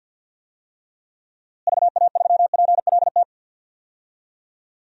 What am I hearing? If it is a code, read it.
Morse code, heard as a keyed tone, about 51 wpm, 700 Hz. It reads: VA3PLT
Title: DXCC-100 Call Signs - 6 Characters - Repeated 3x - 50wpm